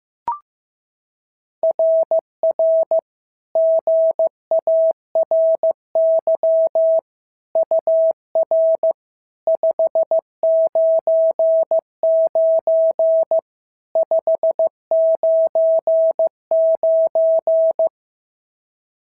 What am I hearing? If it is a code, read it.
Morse code, heard as a keyed tone, about 15 wpm, 650 Hz. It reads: RR GARY UR 599 599
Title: SOTA QSO Practice 15wpm